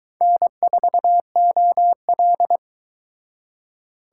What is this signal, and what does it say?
Morse code, heard as a keyed tone, about 22 wpm, 700 Hz. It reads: N4OL